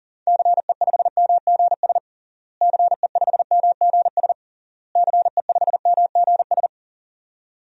Morse code, heard as a keyed tone, about 40 wpm, 700 Hz. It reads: CE5MGS CE5MGS CE5MGS
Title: DXCC-100 Call Signs - 6 Characters - Repeated 3x - 40wpm